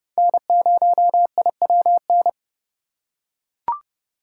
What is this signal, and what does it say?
Morse code, heard as a keyed tone, about 30 wpm, 700 Hz. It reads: N0IWN